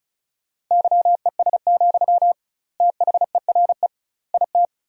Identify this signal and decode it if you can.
Morse code, heard as a keyed tone, about 36 wpm, 700 Hz. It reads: YES, THERE IT
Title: 2x Word-Spacing - Sentences from Top 500 Words - 35wpm